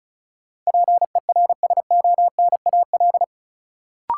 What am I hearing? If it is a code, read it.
Morse code, heard as a keyed tone, about 36 wpm, 700 Hz. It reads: PERSONAL
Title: Sentences from Top 700 Words - New Word Drill - 35wpm